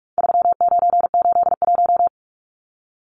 Morse code, heard as a keyed tone, about 44 wpm, 700 Hz. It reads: 3981
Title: Numbers - 4-Digit 45wpm